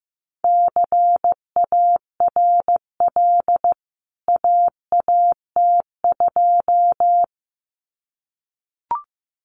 Morse code, heard as a keyed tone, about 15 wpm, 700 Hz. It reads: CARL AAT2
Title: CWT Exchange 15wpm